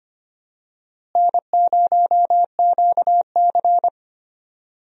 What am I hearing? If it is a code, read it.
Morse code, heard as a keyed tone, about 25 wpm, 700 Hz. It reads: N0QC
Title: USA Call Signs - 4 Characters - Repeated 3x - 25wpm